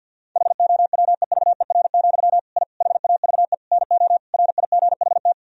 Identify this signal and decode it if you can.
Morse code, heard as a keyed tone, about 49 wpm, 700 Hz. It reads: HOWEVER, I HAVE NO RIGHT
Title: Sentences from Top 1000 Words - Review Part 3 - 50wpm